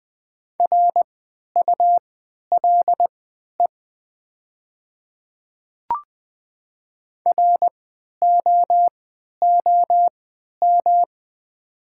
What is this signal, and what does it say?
Morse code, heard as a keyed tone, about 20 wpm, spaced as for 11 wpm, 700 Hz. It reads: RULE ROOM
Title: 3x Character Spacing - Top 500 Words - 20wpm